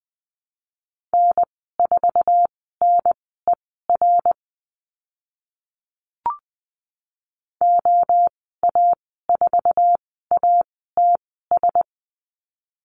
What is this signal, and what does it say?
Morse code, heard as a keyed tone, about 20 wpm, spaced as for 14 wpm, 700 Hz. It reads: N4NER OA4ATS
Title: 082 Lesson 14g - Number 2 - Call Signs 20wpm